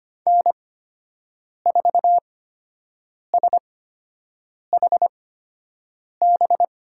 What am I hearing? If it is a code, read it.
Morse code, heard as a keyed tone, about 25 wpm, spaced as for 7 wpm, 700 Hz. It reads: N4SHB
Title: USA Call Signs - 8x Character Spacing - 25wpm